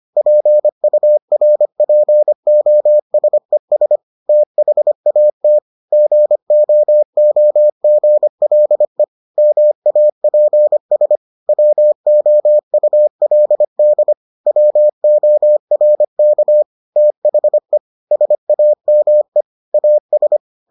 Morse code, heard as a keyed tone, about 25 wpm, 600 Hz. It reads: PURPOSES THAT GOOGLE MAPS WOULD WORK THE SAME AS